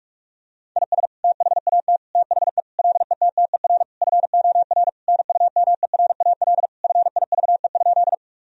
Morse code, heard as a keyed tone, about 43 wpm, 700 Hz. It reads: IS THAT THE LETTER FOR NUMERAL FIVE?
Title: Sentences from Top 1000 Words - Review Part 2 - 45wpm